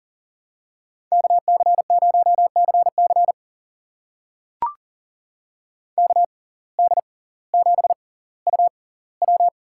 Morse code, heard as a keyed tone, about 41 wpm, spaced as for 21 wpm, 700 Hz. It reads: KC0CC KD7UW
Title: USA Call Signs - 5 Characters - 6x Character Spacing - 40wpm